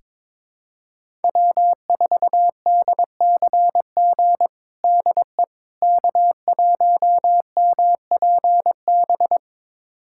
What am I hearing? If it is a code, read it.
Morse code, heard as a keyed tone, about 21 wpm, 700 Hz. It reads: W4DCG DE K1MPB
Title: USA Callsign DE Callsign - 5 Characters - 22wpm